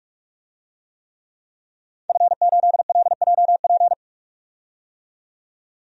Morse code, heard as a keyed tone, about 41 wpm, 700 Hz. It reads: F8LJP
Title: DXCC-100 Call Signs - 5 Characters - 45wpm